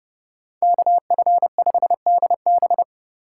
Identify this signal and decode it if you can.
Morse code, heard as a keyed tone, about 39 wpm, 700 Hz. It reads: KF5DB